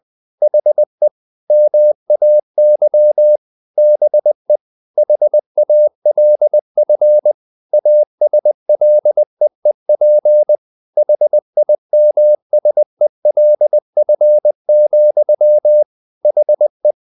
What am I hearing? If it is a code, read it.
Morse code, heard as a keyed tone, about 20 wpm, 600 Hz. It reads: HE MAY BE HALF ASLEEP HIMSELF, HE
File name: the old man and the sea part 2 20wpm